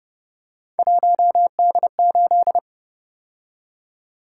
Morse code, heard as a keyed tone, about 29 wpm, 700 Hz. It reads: 1D8
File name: Speed-Racing - Three Letter-Number Combinations - 25wpm